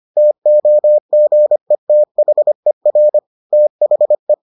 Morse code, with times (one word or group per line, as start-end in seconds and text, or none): 0.2-3.2 TOGETHER
3.5-4.3 THE